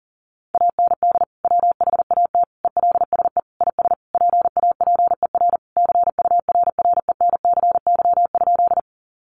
Morse code, heard as a keyed tone, about 40 wpm, 700 Hz. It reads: AND WHAT ELSE IS PAPER CURRENCY?
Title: Sentences from Top 1500 Words 40wpm